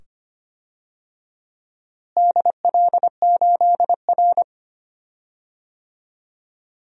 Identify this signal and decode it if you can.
Morse code, heard as a keyed tone, about 25 wpm, 700 Hz. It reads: DL8R
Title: DXCC-100 Call Signs - 4 Characters - Repeated 2x - 25wpm